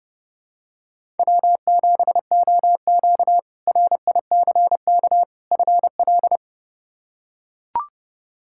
Morse code, heard as a keyed tone, about 30 wpm, 700 Hz. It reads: W7OQ RICK FL
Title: K1USN SST QSO Practice - 30wpm